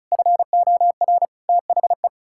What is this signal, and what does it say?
Morse code, heard as a keyed tone, about 36 wpm, 700 Hz. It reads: FOR THE
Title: Pangram Challenge v2 - 35wpm